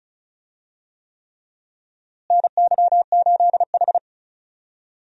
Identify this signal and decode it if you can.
Morse code, heard as a keyed tone, about 36 wpm, 700 Hz. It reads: NY8H
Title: USA Call Signs - 4 Characters - Repeated 3x - 35wpm